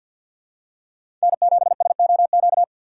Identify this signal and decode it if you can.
Morse code, heard as a keyed tone, about 47 wpm, 700 Hz. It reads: N7SOQ
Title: USA Call Signs - 5 Characters - 6x Character Spacing - 50wpm